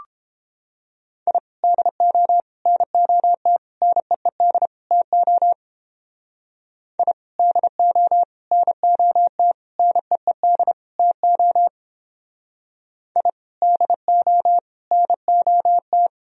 Morse code, heard as a keyed tone, about 29 wpm, 700 Hz. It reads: I DO NOT NEED TO I DO NOT NEED TO I DO NOT